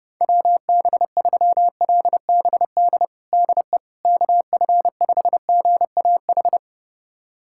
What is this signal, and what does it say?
Morse code, heard as a keyed tone, about 30 wpm, 700 Hz. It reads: WB3LBD DE KF5GAH
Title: USA Callsign DE Callsign 30wpm